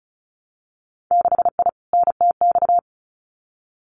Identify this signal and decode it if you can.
Morse code, heard as a keyed tone, about 36 wpm, 700 Hz. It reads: 6I NTX
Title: Winter Field Day Exchanges Encore 35wpm